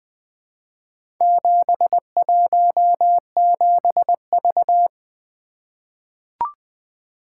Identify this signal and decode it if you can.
Morse code, heard as a keyed tone, about 20 wpm, 700 Hz. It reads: 717V
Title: Four Letter-Number Combinations - 20wpm